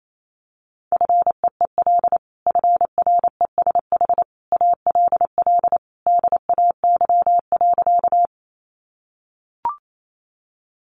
Morse code, heard as a keyed tone, about 28 wpm, 700 Hz. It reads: FEEL FRESH ALL DAY.
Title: Sentences from Top 1200 Words 28wpm